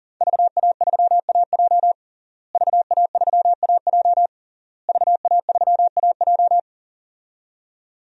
Morse code, heard as a keyed tone, about 37 wpm, 700 Hz. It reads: VA3AJ VA3AJ VA3AJ
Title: DXCC-100 Call Signs - 5 Characters - Repeated 3x - 40wpm